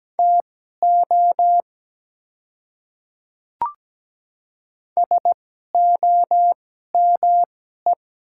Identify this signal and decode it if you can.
Morse code, heard as a keyed tone, about 17 wpm, spaced as for 12 wpm, 700 Hz. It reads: TO SOME
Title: 2x Character Spacing - Top 100 Words - 17wpm